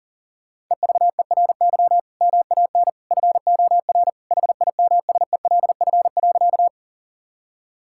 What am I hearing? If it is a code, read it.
Morse code, heard as a keyed tone, about 40 wpm, 700 Hz. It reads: EVERY MAN FOR HIMSELF.
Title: Sentences from Top 1600 Words 40wpm